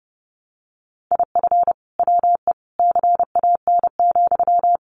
Morse code, heard as a keyed tone, about 30 wpm, 700 Hz. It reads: IF WE CAN,
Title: Sentences from Top 1100 Words 30wpm